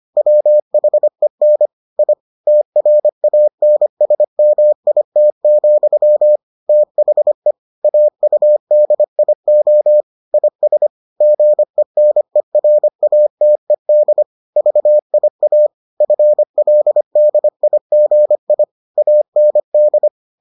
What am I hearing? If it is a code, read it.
Morse code, heard as a keyed tone, about 25 wpm, 600 Hz. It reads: WHEN I TRANSMIT, THE AUDIO IS GENERATED VIA FLDIGI AND